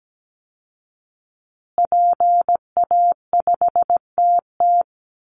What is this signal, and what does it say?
Morse code, heard as a keyed tone, about 17 wpm, 700 Hz. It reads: PA5TT
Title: Call Signs Repeated 2x - 17wpm